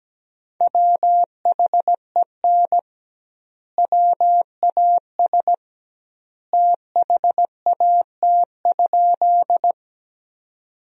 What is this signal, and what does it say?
Morse code, heard as a keyed tone, about 17 wpm, 700 Hz. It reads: WHEN WAS THAT?
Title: 2x Word-Spacing - Sentences from Top 100 Words - 17wpm